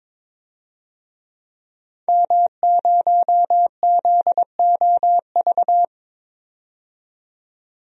Morse code, heard as a keyed tone, about 22 wpm, 700 Hz. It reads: M0ZOV
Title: DXCC-100 Call Signs - 5 Characters - 22wpm